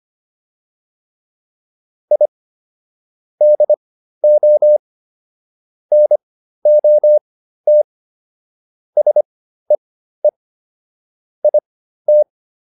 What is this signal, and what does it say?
Morse code, heard as a keyed tone, about 25 wpm, spaced as for 13 wpm, 600 Hz. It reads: I DO NOT SEE IT